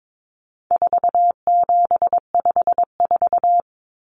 Morse code, heard as a keyed tone, about 22 wpm, 700 Hz. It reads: 4754
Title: Common Cut Numbers - 4-Digit 22wpm